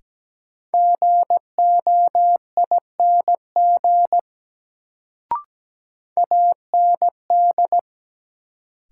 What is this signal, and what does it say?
Morse code, heard as a keyed tone, about 17 wpm, 700 Hz. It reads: GOING AND
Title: Sets of 2 Words - New Word Drill - 17wpm